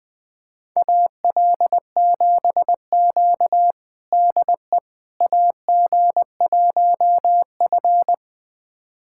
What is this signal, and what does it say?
Morse code, heard as a keyed tone, about 21 wpm, 700 Hz. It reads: AL7Q DE AG1F
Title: USA Callsign DE Callsign - 4 Characters - 20wpm